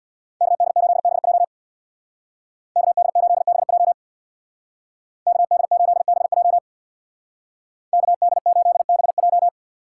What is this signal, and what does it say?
Morse code, heard as a keyed tone, about 60 wpm, 700 Hz. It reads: KD8BJ KD8BJ KD8BJ KD8BJ
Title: Speed Racing - USA 2x2 Call Signs 50wpm